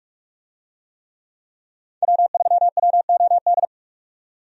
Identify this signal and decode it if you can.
Morse code, heard as a keyed tone, about 46 wpm, 700 Hz. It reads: W3WOD
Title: USA Call Signs - 5 Characters - 45wpm